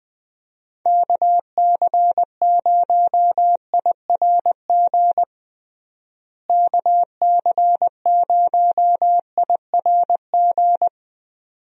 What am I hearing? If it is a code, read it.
Morse code, heard as a keyed tone, about 20 wpm, 700 Hz. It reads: KC0IRG KC0IRG
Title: USA Call Signs - 6 Characters - Repeated 2x - 20wpm